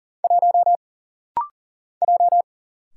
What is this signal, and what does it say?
Morse code, heard as a keyed tone, about 39 wpm, 700 Hz. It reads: 1 J
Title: Single Letter-Number - Rapid-Fire 40wpm